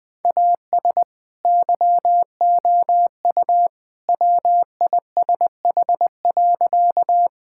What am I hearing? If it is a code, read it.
Morse code, heard as a keyed tone, about 20 wpm, 700 Hz. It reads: AS YOU WISH.
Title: Sentences from Top 1600 Words 20wpm